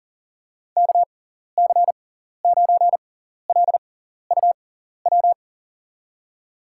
Morse code, heard as a keyed tone, about 40 wpm, spaced as for 14 wpm, 700 Hz. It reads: KC9LUW
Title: USA Call Signs - 6 Characters - 6x Character Spacing - 40wpm